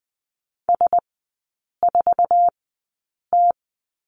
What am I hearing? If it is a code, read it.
Morse code, heard as a keyed tone, about 20 wpm, spaced as for 8 wpm, 700 Hz. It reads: S4T
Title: 047 Lesson 9e - Number 4 - Three Characters 20wpm